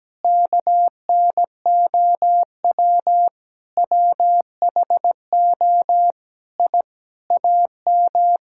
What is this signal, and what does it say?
Morse code, heard as a keyed tone, about 17 wpm, 700 Hz. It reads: KNOW WHO I AM
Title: Sentences from Top 1000 Words - Review Part 5 - 17wpm